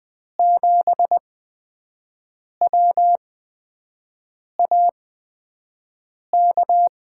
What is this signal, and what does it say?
Morse code, heard as a keyed tone, about 20 wpm, spaced as for 5 wpm, 700 Hz. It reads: 7WAK
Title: USA Call Signs - 5 Characters - 8x Character Spacing - 20wpm